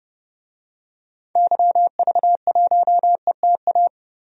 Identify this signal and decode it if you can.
Morse code, heard as a keyed tone, about 30 wpm, 700 Hz. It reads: YV1ETA
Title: DXCC-100 Call Signs - 6 Characters - 30wpm